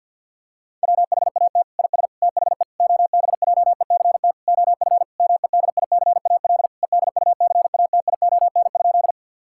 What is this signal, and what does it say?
Morse code, heard as a keyed tone, about 52 wpm, 700 Hz. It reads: WHAT IS THE OBJECT OF MEDICAL EDUCATION?
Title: Sentences from Top 1000 Words - Review Part 6 - 50wpm